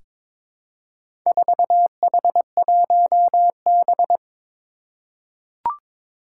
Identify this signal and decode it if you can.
Morse code, heard as a keyed tone, about 22 wpm, 700 Hz. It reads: 4H1B
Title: Four Challenging Letter-Number Combinations 22wpm